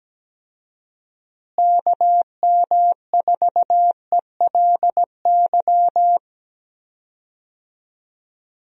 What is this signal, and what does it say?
Morse code, heard as a keyed tone, about 17 wpm, 700 Hz. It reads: KM4ELY
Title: USA Call Signs - 6 Characters - Repeated 2x - 17wpm